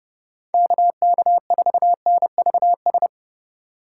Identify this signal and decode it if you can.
Morse code, heard as a keyed tone, about 29 wpm, 700 Hz. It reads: KK4NVS